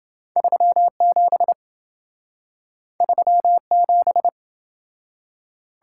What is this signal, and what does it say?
Morse code, heard as a keyed tone, about 27 wpm, 700 Hz. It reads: 37 37